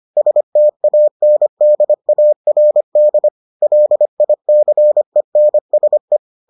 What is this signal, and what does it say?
Morse code, heard as a keyed tone, about 25 wpm, 600 Hz. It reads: STANDARD LICENSE